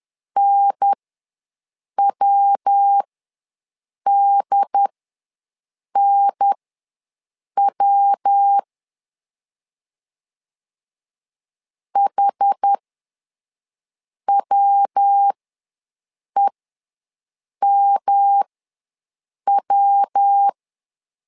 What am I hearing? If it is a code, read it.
Morse code, heard as a keyed tone, about 10 wpm, spaced as for 6 wpm, 800 Hz. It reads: NWDNW HWEMW
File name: LEZIONI / 3 MWRO / 3 MW ws 10wpm 2213